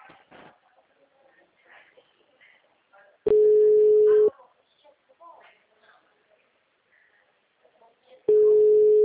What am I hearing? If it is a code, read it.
Ringback tone (Europe, 425 Hz), with speech.
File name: out- -2017- - - .34880